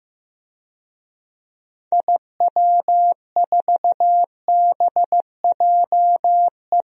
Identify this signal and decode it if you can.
Morse code, heard as a keyed tone, about 15 wpm, 700 Hz. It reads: IW4BJE
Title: DXCC-100 Call Signs - 6 Characters - Repeated 2x - 15wpm